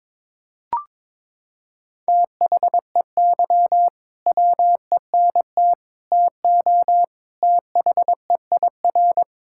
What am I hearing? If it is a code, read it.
Morse code, heard as a keyed tone, about 22 wpm, 700 Hz. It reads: THEY WENT TO THEIR
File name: practice / 22wpm / Sentences with 2-Word Phrases - 22wpm